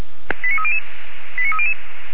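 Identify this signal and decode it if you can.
Five-tone selective call (ZVEI), heard as quick runs of five tones, tones of about 70 ms. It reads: 89300 89300
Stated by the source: ZVEI five-tone selective call received on SDR, from feuerwehr-orlamuende.org (fire brigade Orlamünde).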